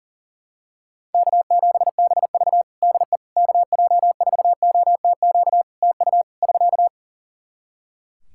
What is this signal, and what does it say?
Morse code, heard as a keyed tone, about 37 wpm, 700 Hz. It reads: K7BV DE KJ4OTQ TU <SK>
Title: Practice QSO - Exchange 4 - Ending - 40wpm